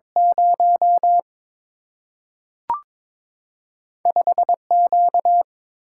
Morse code, heard as a keyed tone, about 23 wpm, 700 Hz. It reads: 0 5Q
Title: Single-Letter Single-Number 22wpm